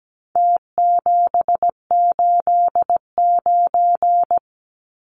Morse code, heard as a keyed tone, about 17 wpm, 700 Hz. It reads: T789
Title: Numbers - Contest Style - 4-Digit 17wpm